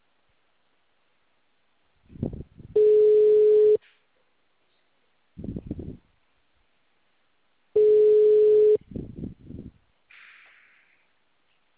Ringback tone (Europe, 425 Hz).